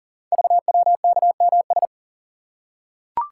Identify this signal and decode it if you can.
Morse code, heard as a keyed tone, about 40 wpm, 690 Hz. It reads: VWKMS
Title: Code Groups - 5 Letters - 40wpm